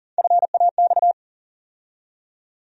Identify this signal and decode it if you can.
Morse code, heard as a keyed tone, about 40 wpm, 700 Hz. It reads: FAX
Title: USA Call Sign Postfixes - 3 Letters 40wpm